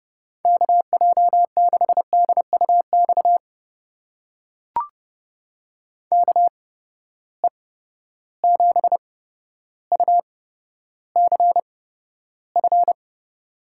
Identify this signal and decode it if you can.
Morse code, heard as a keyed tone, about 30 wpm, spaced as for 13 wpm, 700 Hz. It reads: KJ6DUX KE7UCF
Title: USA Call Signs - 6 Characters - 8x Character Spacing - 30wpm